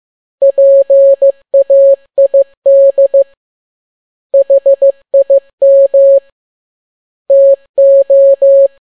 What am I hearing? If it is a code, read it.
Morse code, heard as a keyed tone, about 15 wpm, 550 Hz. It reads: PAID HIM TO
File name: n6tt p1 15C13